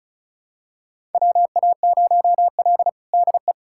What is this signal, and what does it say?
Morse code, heard as a keyed tone, about 36 wpm, 700 Hz. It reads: WA0L DE
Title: USA Callsign DE Callsign - 4 Characters - 35wpm